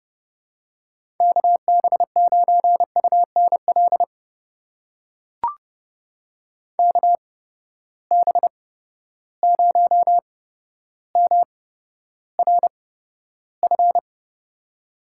Morse code, heard as a keyed tone, about 32 wpm, spaced as for 13 wpm, 700 Hz. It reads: KB9UNL KB0MRF
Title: USA Call Signs - 6 Characters - 8x Character Spacing - 30wpm